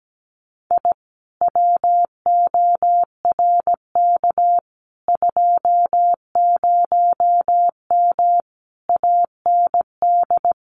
Morse code, heard as a keyed tone, about 17 wpm, 700 Hz. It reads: I WORK 20M AND
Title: Common QSO Phrases Encore 17wpm